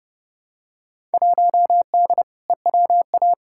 Morse code, heard as a keyed tone, about 30 wpm, 700 Hz. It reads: 1D EWA
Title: Field Day Exchanges - 30wpm